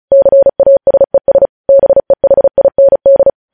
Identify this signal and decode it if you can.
Morse code, heard as a keyed tone, about 36 wpm, 550 Hz. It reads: CASES BEHIND
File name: podcast / News headlines-1x-35x35